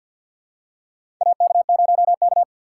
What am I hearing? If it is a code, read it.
Morse code, heard as a keyed tone, about 48 wpm, 700 Hz. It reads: AK0K
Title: USA Call Signs - 4 Characters - 2x Character Spacing - 50wpm